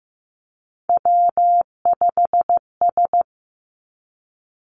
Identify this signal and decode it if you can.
Morse code, heard as a keyed tone, about 15 wpm, 700 Hz. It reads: W5S
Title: Three Challenging Letter-Number Combinations 15wpm